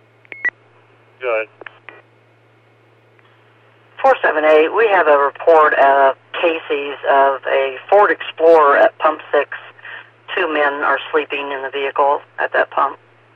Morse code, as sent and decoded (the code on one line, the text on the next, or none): none